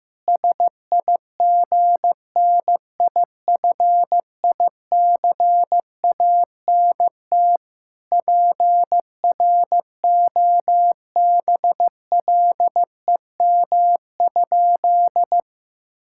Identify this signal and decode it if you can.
Morse code, heard as a keyed tone, about 15 wpm, 700 Hz. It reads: SIGNIFICANT PROBLEM?
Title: Sentences from Top 1000 Words - Review Part 6 - 15wpm